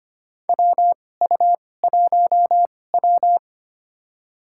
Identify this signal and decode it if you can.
Morse code, heard as a keyed tone, about 25 wpm, spaced as for 18 wpm, 700 Hz. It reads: WU1W
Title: USA Call Signs - 4 Characters - 2x Character Spacing - 25wpm